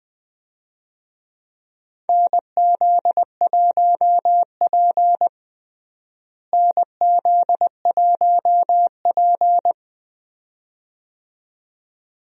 Morse code, heard as a keyed tone, about 20 wpm, 700 Hz. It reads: NZ1P NZ1P
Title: USA Call Signs - 4 Characters - Repeated 2x - 20wpm